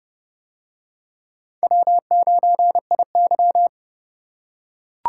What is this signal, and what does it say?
Morse code, heard as a keyed tone, about 27 wpm, 700 Hz. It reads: W9IY